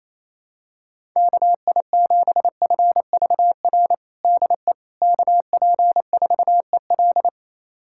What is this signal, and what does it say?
Morse code, heard as a keyed tone, about 27 wpm, 700 Hz. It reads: KI7FVR DE KP4EL
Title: USA Callsign DE Callsign 28wpm